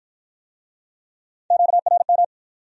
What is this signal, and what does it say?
Morse code, heard as a keyed tone, about 51 wpm, 700 Hz. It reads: QRM